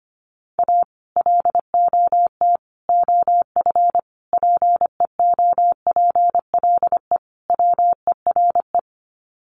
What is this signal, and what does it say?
Morse code, heard as a keyed tone, about 25 wpm, 700 Hz. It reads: A LOT OF PEOPLE WERE
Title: Sets of 5 Words Encore2 25wpm